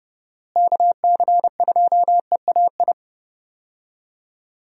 Morse code, heard as a keyed tone, about 30 wpm, 700 Hz. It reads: KC2EAI
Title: USA Call Signs - 6 Characters - Repeated 2x - 30wpm